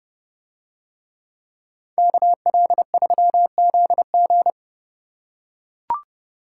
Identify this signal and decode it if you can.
Morse code, heard as a keyed tone, about 30 wpm, 700 Hz. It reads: KL3ZG